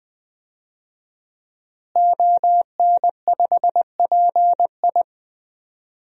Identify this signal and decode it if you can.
Morse code, heard as a keyed tone, about 20 wpm, 700 Hz. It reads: ON5PI
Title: DXCC-100 Call Signs - 5 Characters - Repeated 3x - 20wpm